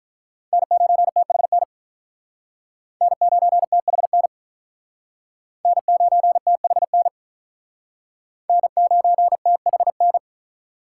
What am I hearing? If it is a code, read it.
Morse code, heard as a keyed tone, about 48 wpm, 700 Hz. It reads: N9THN N9THN N9THN N9THN